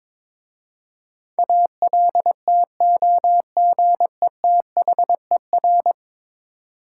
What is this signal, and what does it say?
Morse code, heard as a keyed tone, about 22 wpm, 700 Hz. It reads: ALTOGETHER
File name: practice / 22wpm / 10 Letter Words - First 3 Letters Given - 22wpm